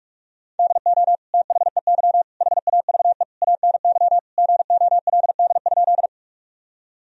Morse code, heard as a keyed tone, about 46 wpm, 700 Hz. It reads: DO THEY HAVE ANY GOLD?
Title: Sentences from Top 1000 Words - Review Part 3 - 45wpm